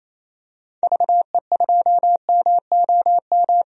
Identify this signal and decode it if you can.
Morse code, heard as a keyed tone, about 28 wpm, 700 Hz. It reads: VE2MOM
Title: DXCC-100 Call Signs - 6 Characters - 28wpm